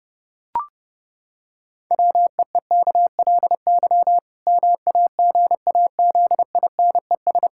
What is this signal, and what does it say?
Morse code, heard as a keyed tone, about 29 wpm, 700 Hz. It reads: WEEKLY MAGAZINES
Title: Pangram Challenge v2 - 30wpm